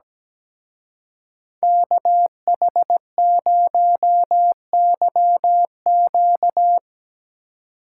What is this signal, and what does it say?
Morse code, heard as a keyed tone, about 20 wpm, 700 Hz. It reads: KH0YQ
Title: Speed Racing - USA 2x2 Call Signs 17wpm